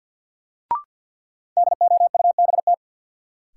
Morse code, heard as a keyed tone, about 53 wpm, 700 Hz. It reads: DOUBT
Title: Sets of 5 Words - New Word Drill - 50wpm